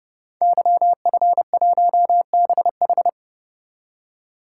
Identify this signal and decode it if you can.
Morse code, heard as a keyed tone, about 30 wpm, 700 Hz. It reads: YF1BH
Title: Code Groups - 5 Letter:Numbers - 30wpm